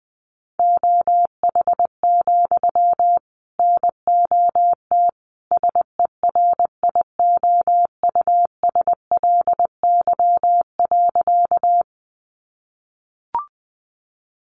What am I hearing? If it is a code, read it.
Morse code, heard as a keyed tone, about 20 wpm, 700 Hz. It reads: OH, NOT SERIOUSLY.
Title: Sentences from Top 1200 Words 20wpm